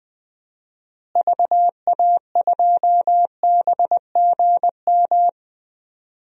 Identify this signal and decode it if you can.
Morse code, heard as a keyed tone, about 20 wpm, 700 Hz. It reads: VA2BGM
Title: DXCC-100 Call Signs - 6 Characters - Repeated 2x - 20wpm